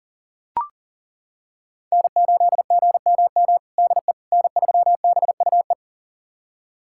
Morse code, heard as a keyed tone, about 37 wpm, 700 Hz. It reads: N8GMM DE N3BUE
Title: USA Callsign DE Callsign - 5 Characters - 40wpm